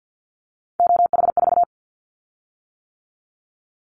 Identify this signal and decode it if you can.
Morse code, heard as a keyed tone, about 50 wpm, 700 Hz. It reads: OH4
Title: Callsign Prefixes for Top 25 DXCC - 50wpm